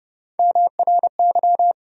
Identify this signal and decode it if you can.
Morse code, heard as a keyed tone, about 30 wpm, 700 Hz. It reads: MRY
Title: USA Call Sign Postfixes - 3 Letters 30wpm